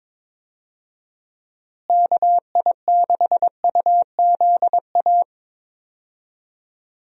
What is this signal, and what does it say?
Morse code, heard as a keyed tone, about 22 wpm, 700 Hz. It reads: KI6UZA
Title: USA Call Signs - 6 Characters - Repeated 2x - 22wpm